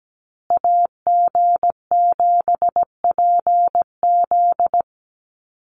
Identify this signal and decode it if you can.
Morse code, heard as a keyed tone, about 17 wpm, 700 Hz. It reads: AG7PZ
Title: USA Call Signs Repeated 2x 17wpm